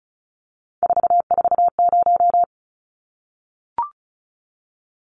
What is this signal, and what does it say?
Morse code, heard as a keyed tone, about 34 wpm, 700 Hz. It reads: 440
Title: Numbers - 3-Digit 35wpm